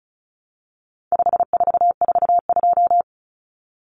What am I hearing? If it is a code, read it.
Morse code, heard as a keyed tone, about 34 wpm, 700 Hz. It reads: 5442
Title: Numbers - 4-Digit 35wpm